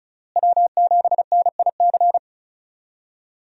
Morse code, heard as a keyed tone, about 36 wpm, 700 Hz. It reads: W7NIC
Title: USA Call Signs - 5 Characters - Repeated 3x - 35wpm